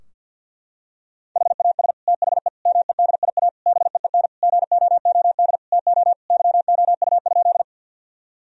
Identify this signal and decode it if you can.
Morse code, heard as a keyed tone, about 49 wpm, 700 Hz. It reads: HAS THE MEDIA BEEN GOOD TO YOU?
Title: Sentences from Top 1000 Words - Review Part 4 - 50wpm